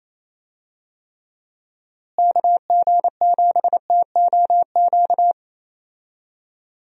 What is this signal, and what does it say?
Morse code, heard as a keyed tone, about 27 wpm, 700 Hz. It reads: KG7TOQ
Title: USA Call Signs - 6 Characters - Repeated 3x - 28wpm